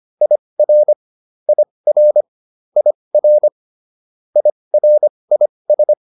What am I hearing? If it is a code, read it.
Morse code, heard as a keyed tone, about 25 wpm, spaced as for 20 wpm, 600 Hz. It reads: IR IR IR IRIS